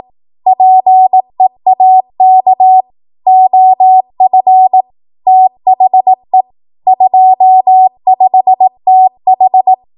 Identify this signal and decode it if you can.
Morse code, heard as a keyed tone, about 18 wpm, 750 Hz. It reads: PEAK OF THE 25TH
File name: Morse / 18WPM